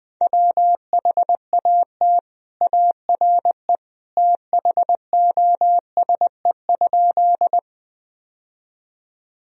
Morse code, heard as a keyed tone, about 20 wpm, 700 Hz. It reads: WHAT ARE THOSE?
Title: Sentences from Top 1000 Words - Review Part 1 - 20wpm